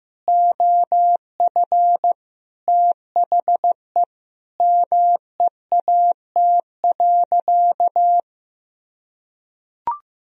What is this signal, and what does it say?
Morse code, heard as a keyed tone, about 15 wpm, 700 Hz. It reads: OF THE MEAT.
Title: Sentences from Top 1000 Words - Review Part 3 - 15wpm